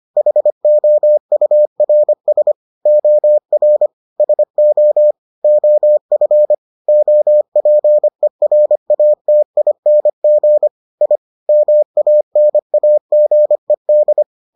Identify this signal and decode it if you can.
Morse code, heard as a keyed tone, about 25 wpm, 600 Hz. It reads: HOURS OR SO OF OPERATING I MANAGED